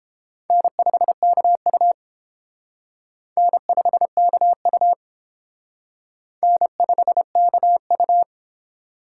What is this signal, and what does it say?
Morse code, heard as a keyed tone, about 29 wpm, 700 Hz. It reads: N5KU N5KU N5KU